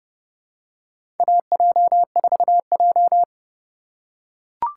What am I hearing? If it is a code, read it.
Morse code, heard as a keyed tone, about 30 wpm, 700 Hz. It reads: AJ4J
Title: USA Call Signs - 4 Characters - 2x Character Spacing - 30wpm